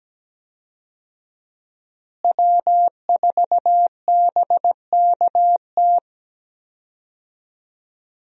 Morse code, heard as a keyed tone, about 17 wpm, 700 Hz. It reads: W4BKT
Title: USA Call Signs - 5 Characters - Repeated 2x - 17wpm